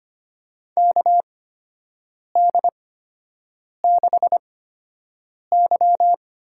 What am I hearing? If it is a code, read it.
Morse code, heard as a keyed tone, about 25 wpm, spaced as for 7 wpm, 700 Hz. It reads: KD6Y
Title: USA Call Signs - 6 Characters - 8x Character Spacing - 25wpm